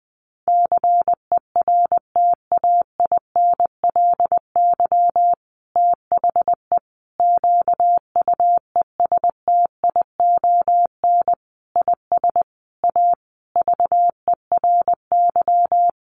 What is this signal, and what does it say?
Morse code, heard as a keyed tone, about 20 wpm, 700 Hz. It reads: CERTAINLY THE QUESTION IS A VERY
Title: Sentences from Top 1200 Words 20wpm